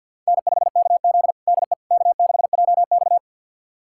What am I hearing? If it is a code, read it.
Morse code, heard as a keyed tone, about 53 wpm, 700 Hz. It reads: N5KZ DE K6JX
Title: USA Callsign DE Callsign - 4 Characters - 50wpm